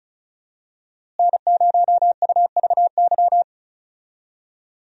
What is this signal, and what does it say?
Morse code, heard as a keyed tone, about 36 wpm, 700 Hz. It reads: N0UVY
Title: USA Call Signs - 5 Characters - Repeated 2x - 35wpm